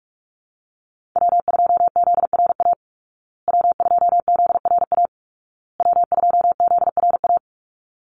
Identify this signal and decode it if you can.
Morse code, heard as a keyed tone, about 48 wpm, 700 Hz. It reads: W2ZRA W2ZRA W2ZRA
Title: Call Signs Repeated 2x - 45wpm